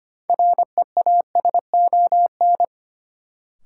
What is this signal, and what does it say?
Morse code, heard as a keyed tone, about 25 wpm, 700 Hz. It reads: REASON
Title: Sets of 5 Words Encore - New Word Drill - 25wpm